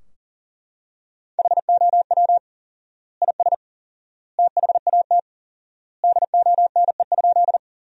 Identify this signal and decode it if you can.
Morse code, heard as a keyed tone, about 40 wpm, 700 Hz. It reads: HOW IS THAT DONE?
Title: 4x Word-Spacing - Sentences from Top 500 Words - 40wpm